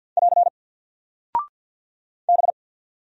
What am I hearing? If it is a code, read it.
Morse code, heard as a keyed tone, about 48 wpm, 700 Hz. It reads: <AR> B
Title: Single Character (Letter:Number:Punctuation:Prosign) Alt 50wpm